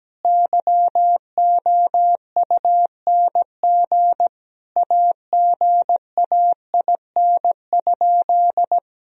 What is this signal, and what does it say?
Morse code, heard as a keyed tone, about 17 wpm, 700 Hz. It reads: YOUNG AGAIN?
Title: Sentences from Top 1000 Words - Review Part 1 - 17wpm